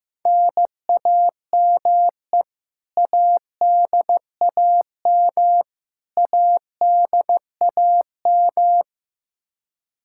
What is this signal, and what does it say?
Morse code, heard as a keyed tone, about 15 wpm, 700 Hz. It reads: NAME ADAM ADAM
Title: K1USN SST QSO Practice 15wpm